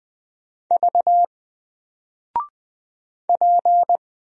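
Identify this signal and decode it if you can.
Morse code, heard as a keyed tone, about 20 wpm, 700 Hz. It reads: V P
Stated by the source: Single letter and number Morse drill at 20 wpm, from N4LCD.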